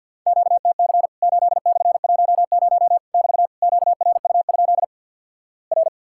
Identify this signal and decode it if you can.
Morse code, heard as a keyed tone, about 50 wpm, 700 Hz. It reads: QTC 8/10 <BT> QRU? R
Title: Worked All Europe DX Contest - QTC Exchange with Repeats - 50wpm